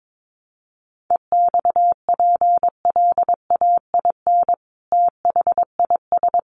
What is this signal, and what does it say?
Morse code, heard as a keyed tone, about 22 wpm, 700 Hz. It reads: EXPLAIN THIS
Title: Sentences from Top 1200 Words 22wpm